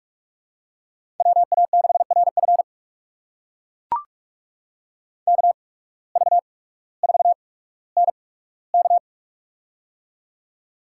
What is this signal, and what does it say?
Morse code, heard as a keyed tone, about 44 wpm, spaced as for 19 wpm, 690 Hz. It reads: WA6RF KV4NK
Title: USA Call Signs - 5 Characters - 8x Character Spacing - 45wpm